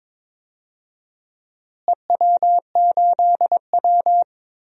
Morse code, heard as a keyed tone, about 22 wpm, 700 Hz. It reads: EW8W
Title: DXCC-100 Call Signs - 4 Characters - Repeated 3x - 22wpm